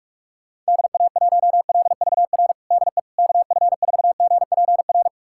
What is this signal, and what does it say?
Morse code, heard as a keyed tone, about 43 wpm, 690 Hz. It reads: DA1LVR DE KF4GPR
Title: USA Callsign DE Callsign - 6 Characters - 45wpm